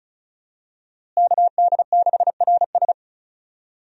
Morse code, heard as a keyed tone, about 36 wpm, 700 Hz. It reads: KD6RS
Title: USA Call Signs - 5 Characters - Repeated 3x - 35wpm